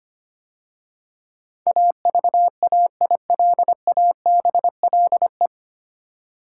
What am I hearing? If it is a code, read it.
Morse code, heard as a keyed tone, about 25 wpm, 690 Hz. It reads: AVAILABLE
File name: practice / 25wpm / 9 Letter Words - First 3 Letters Given - 25wpm